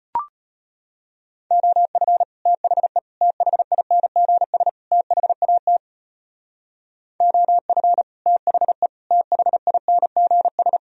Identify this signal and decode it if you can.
Morse code, heard as a keyed tone, about 38 wpm, 700 Hz. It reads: OF THE THINGS THAT OF THE THINGS